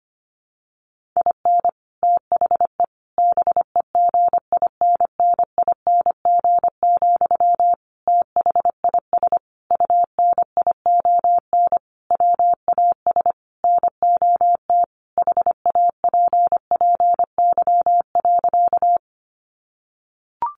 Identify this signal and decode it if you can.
Morse code, heard as a keyed tone, about 25 wpm, 700 Hz. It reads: IN THE BEGINNING, THIS UNION WAS NOT HAPPY.
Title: Sentences from Top 1400 Words 25wpm